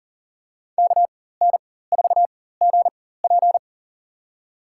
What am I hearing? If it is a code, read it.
Morse code, heard as a keyed tone, about 40 wpm, spaced as for 19 wpm, 700 Hz. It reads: KN4GP
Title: USA Call Signs - 4x Character Spacing - 40wpm